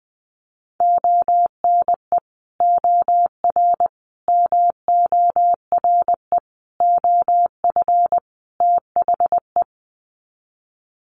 Morse code, heard as a keyed tone, about 20 wpm, 700 Hz. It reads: ONE OR MORE OF THE
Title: Sets of 5 Words Encore 20wpm